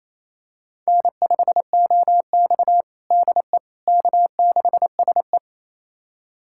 Morse code, heard as a keyed tone, about 28 wpm, 700 Hz. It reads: N5OX DE K6SE
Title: USA Callsign DE Callsign - 4 Characters - 28wpm